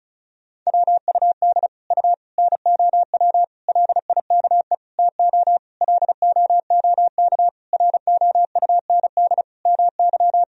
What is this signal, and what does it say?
Morse code, heard as a keyed tone, about 34 wpm, 700 Hz. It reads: WUD U NOW LIKE TO LOOK ROUND MY
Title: Sentences from Top 200 Words and Abbreviations 35wpm